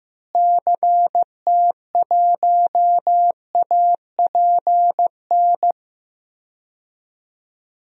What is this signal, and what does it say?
Morse code, heard as a keyed tone, about 15 wpm, 700 Hz. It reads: CT1APN
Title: DXCC-100 Call Signs - 6 Characters - Repeated 2x - 15wpm